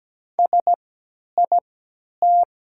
Morse code, heard as a keyed tone, about 17 wpm, spaced as for 10 wpm, 700 Hz. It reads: SIT
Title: 3x Character Spacing - Top 500 Words - 17wpm